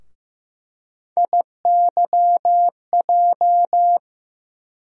Morse code, heard as a keyed tone, about 15 wpm, 700 Hz. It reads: IYJ
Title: USA Call Sign Postfixes - 3 Letters 15wpm